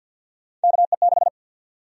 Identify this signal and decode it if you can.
Morse code, heard as a keyed tone, about 52 wpm, 700 Hz. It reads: KE6
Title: USA Call Sign Prefixes - 50wpm